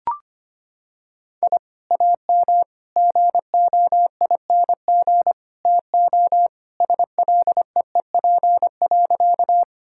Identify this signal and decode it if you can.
Morse code, heard as a keyed tone, about 25 wpm, 700 Hz. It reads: I AM GOING TO SLEEP.
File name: practice / 25wpm / Sentences with 2-Word Phrases - 25wpm